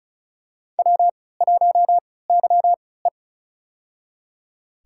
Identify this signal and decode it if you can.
Morse code, heard as a keyed tone, about 34 wpm, spaced as for 20 wpm, 700 Hz. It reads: W1YE
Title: USA Call Signs - 4 Characters - 3x Character Spacing - 35wpm